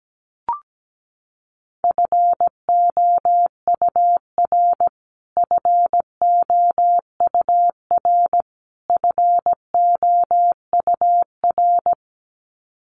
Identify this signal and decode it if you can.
Morse code, heard as a keyed tone, about 17 wpm, 700 Hz. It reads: FOUR FOUR FOUR
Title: Top 500 Words Repeated 17wpm